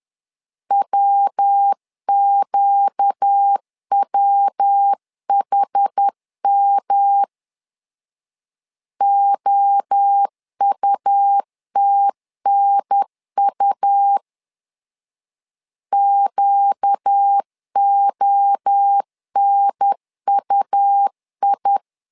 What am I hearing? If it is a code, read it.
Morse code, heard as a keyed tone, about 10 wpm, 800 Hz. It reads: WQWHM OUTNU QONUI
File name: LEZIONI / 4 Q 10wpm 2210